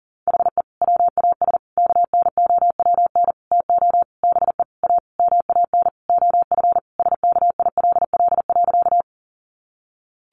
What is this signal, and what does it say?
Morse code, heard as a keyed tone, about 40 wpm, 700 Hz. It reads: HE WAS KNOWN TO BE A MAN OF SKILL.
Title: Sentences from Top 1200 Words 40wpm